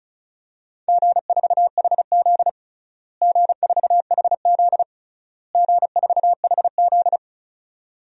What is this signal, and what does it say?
Morse code, heard as a keyed tone, about 34 wpm, 700 Hz. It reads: G4HZ G4HZ G4HZ
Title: DXCC-100 Call Signs - 4 Characters - Repeated 3x - 35wpm